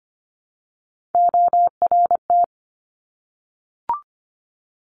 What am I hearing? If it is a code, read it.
Morse code, heard as a keyed tone, about 25 wpm, 700 Hz. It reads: ORT
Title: Top Three Letter Combinations in Words 25wpm